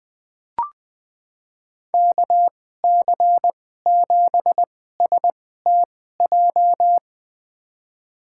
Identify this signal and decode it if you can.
Morse code, heard as a keyed tone, about 20 wpm, spaced as for 14 wpm, 700 Hz. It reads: KC7STJ
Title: USA Call Signs - 6 Characters - 2x Character Spacing - 20wpm